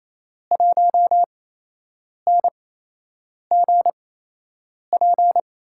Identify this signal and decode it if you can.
Morse code, heard as a keyed tone, about 28 wpm, spaced as for 8 wpm, 700 Hz. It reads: 1NGP
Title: USA Call Signs - 5 Characters - 8x Character Spacing - 28wpm